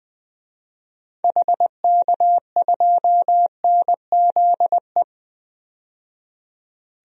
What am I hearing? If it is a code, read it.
Morse code, heard as a keyed tone, about 20 wpm, 700 Hz. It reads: HK2NZE
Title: DXCC-100 Call Signs - 6 Characters - 20wpm